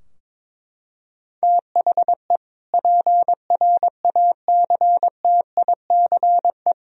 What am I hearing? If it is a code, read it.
Morse code, heard as a keyed tone, about 22 wpm, 700 Hz. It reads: THE PRACTICE
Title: Sentences from Top 1000 Words - Review Part 4 - 22wpm